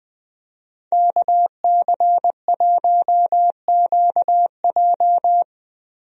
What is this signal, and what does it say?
Morse code, heard as a keyed tone, about 21 wpm, 700 Hz. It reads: KC1QJ
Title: USA Call Signs - 5 Characters - Repeated 2x - 20wpm